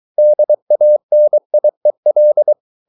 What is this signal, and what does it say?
Morse code, heard as a keyed tone, about 23 wpm, 600 Hz. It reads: DANIEL
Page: Ferie 23 wpm